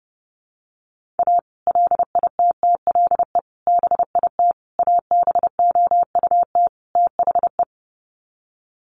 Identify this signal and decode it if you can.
Morse code, heard as a keyed tone, about 30 wpm, 700 Hz. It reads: A LITTLE BIT ABOUT THE
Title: Sets of 5 Words Encore2 30wpm